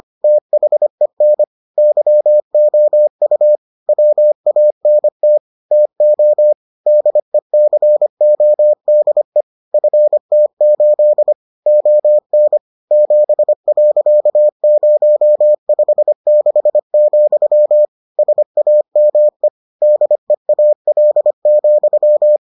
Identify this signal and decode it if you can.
Morse code, heard as a keyed tone, about 25 wpm, 600 Hz. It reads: THEN YOU WANT TO DECODE FT8 ON 7.056, SAME DEAL,